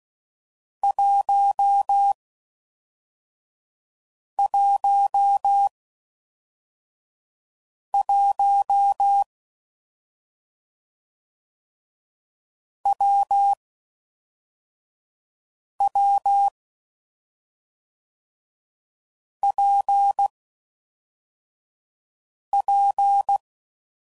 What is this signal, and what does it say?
Morse code, heard as a keyed tone, about 16 wpm, 800 Hz. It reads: 1 1 1 W W P P
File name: Combinations / Difficult Combos / WPJ WPJ1